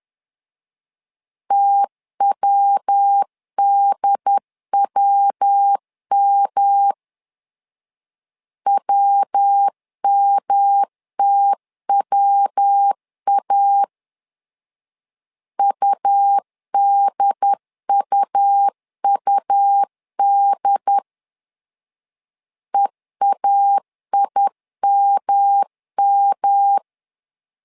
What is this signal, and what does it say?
Morse code, heard as a keyed tone, about 10 wpm, 800 Hz. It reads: TWDWM WMTWA UDUUD EAIMM
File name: LEZIONI / 3 MWRO / 3 MW 10wpm 1959